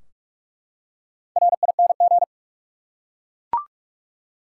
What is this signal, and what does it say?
Morse code, heard as a keyed tone, about 43 wpm, 700 Hz. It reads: RING
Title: Four Letter Words - 45wpm